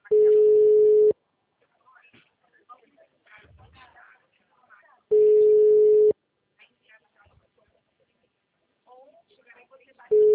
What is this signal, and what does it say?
Ringback tone (Europe, 425 Hz), with speech.
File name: out- -2016- - - .41166